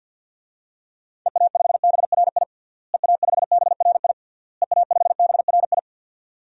Morse code, heard as a keyed tone, about 52 wpm, 700 Hz. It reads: EA5BRI EA5BRI EA5BRI
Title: DXCC-100 Call Signs - 6 Characters - Repeated 3x - 50wpm